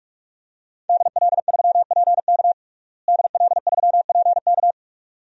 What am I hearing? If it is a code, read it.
Morse code, heard as a keyed tone, about 46 wpm, 700 Hz. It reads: DL3PK DL3PK
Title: DXCC-100 Call Signs - 5 Characters - Repeated 2x - 45wpm